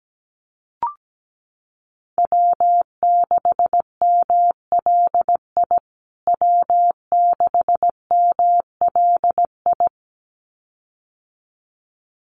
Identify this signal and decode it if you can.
Morse code, heard as a keyed tone, about 17 wpm, 700 Hz. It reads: W6MLI W6MLI
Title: USA Call Signs Repeated 2x 17wpm